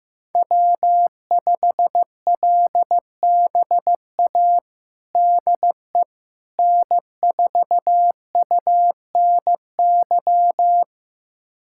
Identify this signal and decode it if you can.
Morse code, heard as a keyed tone, about 15 wpm, 700 Hz. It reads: W5LBA DE N4UNY
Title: USA Callsign DE Callsign - 5 Characters - 15wpm